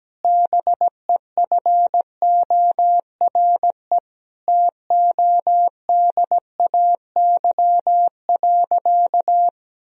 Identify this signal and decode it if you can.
Morse code, heard as a keyed tone, about 17 wpm, 700 Hz. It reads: BEFORE TODAY.
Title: Sentences from Top 1000 Words - Review Part 4 - 17wpm